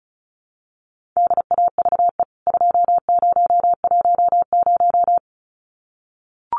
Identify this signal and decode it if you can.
Morse code, heard as a keyed tone, about 34 wpm, 700 Hz. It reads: DAVE 2010
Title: CWT Exchange 35wpm